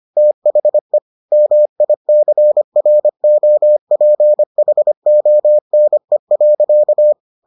Morse code, heard as a keyed tone, about 25 wpm, 600 Hz. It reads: THE MICROPHONE.